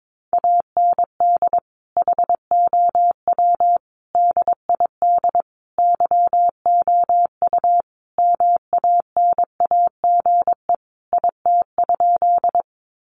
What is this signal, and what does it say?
Morse code, heard as a keyed tone, about 22 wpm, 700 Hz. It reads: AND HOW DID YOU MANAGE IT?
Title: Sentences from Top 1400 Words 22wpm